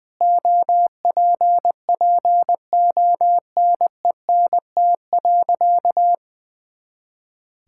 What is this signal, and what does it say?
Morse code, heard as a keyed tone, about 20 wpm, 700 Hz. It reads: OPPONENT.
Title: Pangram Challenge v2 - 20wpm